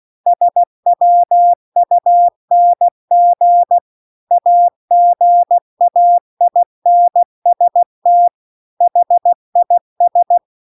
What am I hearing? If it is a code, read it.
Morse code, heard as a keyed tone, about 16 wpm, 700 Hz. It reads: SWUNG AGAINST HIS
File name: CW-The-Big-Twoheaded-River-2-16WPM